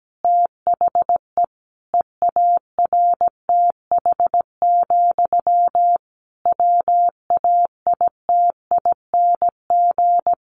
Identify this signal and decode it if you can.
Morse code, heard as a keyed tone, about 17 wpm, 700 Hz. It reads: THE EARTH, WAITING
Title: A Sound Of Thunder 17wpm